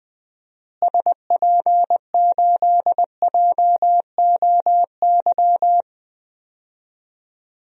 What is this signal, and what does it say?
Morse code, heard as a keyed tone, about 20 wpm, 700 Hz. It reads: SP8JOY
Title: DXCC-100 Call Signs - 6 Characters - Repeated 2x - 20wpm